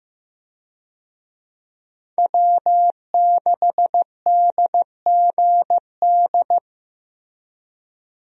Morse code, heard as a keyed tone, about 15 wpm, 700 Hz. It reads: W6DGD
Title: USA Call Signs - 5 Characters - Repeated 3x - 15wpm